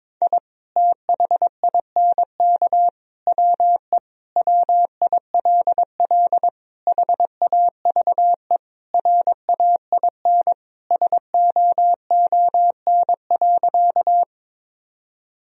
Morse code, heard as a keyed tone, about 22 wpm, 700 Hz. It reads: I THINK WE WILL HAVE RAIN SOON.
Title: Sentences from Top 1000 Words - Review Part 1 - 22wpm